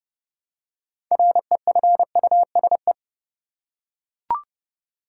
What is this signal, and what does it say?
Morse code, heard as a keyed tone, about 30 wpm, 700 Hz. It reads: REFUSE
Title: Six Letter Words - 30wpm